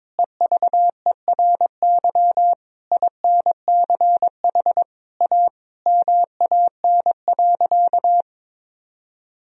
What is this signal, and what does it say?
Morse code, heard as a keyed tone, about 22 wpm, 700 Hz. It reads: EVERY INCH A MAN.
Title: Sentences from Top 1000 Words - Review Part 3 - 22wpm